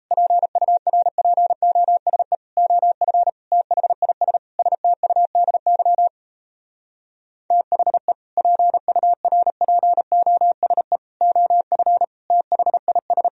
Morse code, heard as a keyed tone, about 38 wpm, 700 Hz. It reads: PURPOSE OF THIS STUDY THE PURPOSE OF THIS